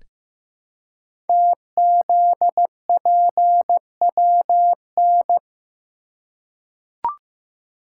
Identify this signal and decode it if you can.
Morse code, heard as a keyed tone, about 15 wpm, 700 Hz. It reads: TZPWN
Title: Code Groups - 5 Letter:Numbers - 15wpm